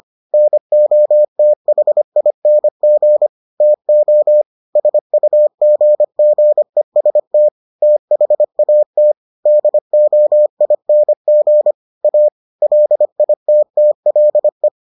Morse code, heard as a keyed tone, about 25 wpm, 600 Hz. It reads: NOTHING TO SUGGEST THAT DOING A LITTLE